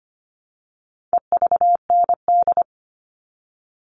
Morse code, heard as a keyed tone, about 25 wpm, 700 Hz. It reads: EVND
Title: All Cut Numbers - 4-Digit 25wpm